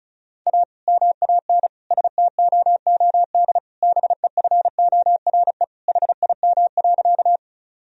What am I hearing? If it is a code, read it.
Morse code, heard as a keyed tone, about 36 wpm, 700 Hz. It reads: A MAN STOOD BEFORE HIM.
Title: Sentences from Top 1000 Words - Review Part 2 - 35wpm